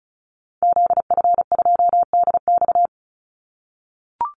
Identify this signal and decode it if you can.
Morse code, heard as a keyed tone, about 34 wpm, 700 Hz. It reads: ZF2DX
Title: Call Signs - 35wpm